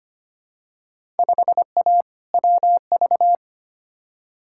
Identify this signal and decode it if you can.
Morse code, heard as a keyed tone, about 25 wpm, 700 Hz. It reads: 5A WV
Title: Field Day Exchanges - 25wpm